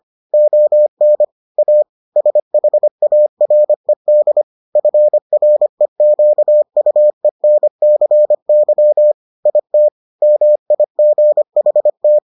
Morse code, heard as a keyed tone, about 25 wpm, 600 Hz. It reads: ON A SHARED FREQUENCY IT MIGHT